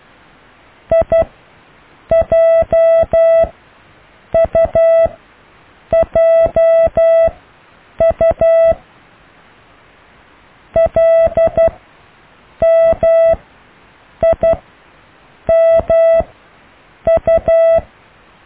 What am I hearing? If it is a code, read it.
Morse code, heard as a keyed tone, about 12 wpm, spaced as for 7 wpm, 660 Hz. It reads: IJUJU LMIMU
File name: cw / IJLMU